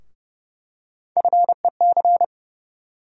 Morse code, heard as a keyed tone, about 30 wpm, 700 Hz. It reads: FEC
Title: USA Call Sign Postfixes - 3 Letters 30wpm